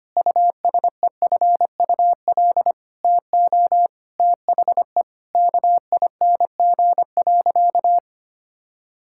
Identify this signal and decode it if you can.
Morse code, heard as a keyed tone, about 25 wpm, 700 Hz. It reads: USEFUL TO THE KING.
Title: Sentences from Top 1000 Words - Review Part 6 - 25wpm